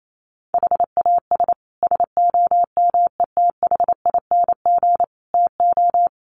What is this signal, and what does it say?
Morse code, heard as a keyed tone, about 28 wpm, 700 Hz. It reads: HAS SOMETHING TO
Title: Sets of 5 Words Encore 28wpm